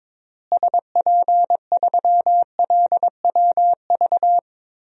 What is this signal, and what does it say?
Morse code, heard as a keyed tone, about 22 wpm, 700 Hz. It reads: SP3LWV
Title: DXCC-100 Call Signs - 6 Characters - Repeated 2x - 22wpm